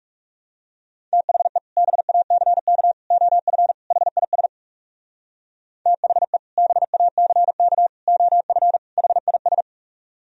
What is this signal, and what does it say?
Morse code, heard as a keyed tone, about 39 wpm, 700 Hz. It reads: THE BACK OF HIS THE BACK OF HIS